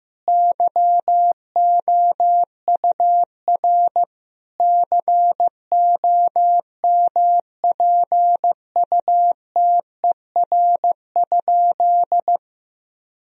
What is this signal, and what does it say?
Morse code, heard as a keyed tone, about 15 wpm, 700 Hz. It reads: YOUR COMPUTER?
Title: Sentences from Top 1000 Words - Review Part 3 - 15wpm